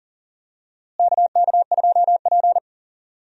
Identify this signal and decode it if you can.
Morse code, heard as a keyed tone, about 40 wpm, 700 Hz. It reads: KK2P
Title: USA Call Signs - 4 Characters - Repeated 3x - 40wpm